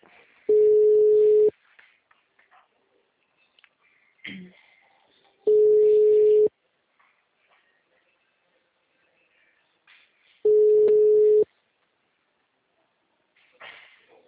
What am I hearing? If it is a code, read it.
Ringback tone (Europe, 425 Hz).
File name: out- -2003- - - .57557